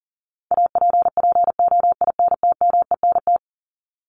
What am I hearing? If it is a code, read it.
Morse code, heard as a keyed tone, about 40 wpm, 700 Hz. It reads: APPOINTMENT
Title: Top 1751-2000 Words 40wpm